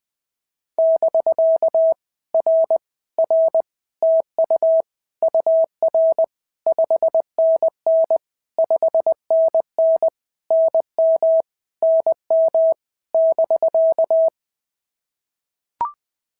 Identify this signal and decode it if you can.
Morse code, heard as a keyed tone, about 20 wpm, 650 Hz. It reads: <BK> R R TU UR 5NN 5NN NM NM <BK>
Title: POTA QSO Practice - 20wpm